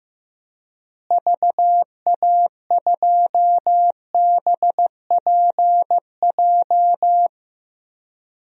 Morse code, heard as a keyed tone, about 15 wpm, 700 Hz. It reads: VA2BPJ
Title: DXCC-100 Call Signs - 6 Characters - Repeated 3x - 15wpm